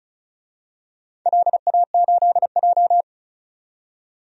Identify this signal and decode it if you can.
Morse code, heard as a keyed tone, about 34 wpm, 700 Hz. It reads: LA8J
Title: DXCC-100 Call Signs - 4 Characters - 35wpm